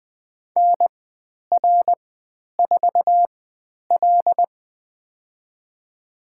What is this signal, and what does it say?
Morse code, heard as a keyed tone, about 20 wpm, spaced as for 10 wpm, 700 Hz. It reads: NR4L
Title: 095 Lesson 16g - Letter c - Call Signs 20wpm